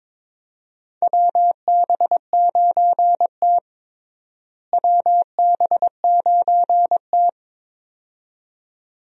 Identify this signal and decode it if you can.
Morse code, heard as a keyed tone, about 22 wpm, 700 Hz. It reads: WB9T WB9T
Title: USA Call Signs - 4 Characters - Repeated 2x - 22wpm